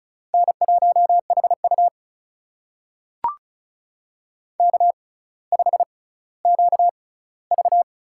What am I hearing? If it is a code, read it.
Morse code, heard as a keyed tone, about 36 wpm, spaced as for 18 wpm, 700 Hz. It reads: N1HU K5QV
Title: USA Call Signs - 4 Characters - 6x Character Spacing - 35wpm